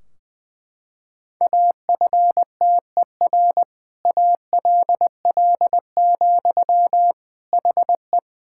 Morse code, heard as a keyed tone, about 21 wpm, 700 Hz. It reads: AFTER ALL, HE
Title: Sentences from Top 1600 Words 20wpm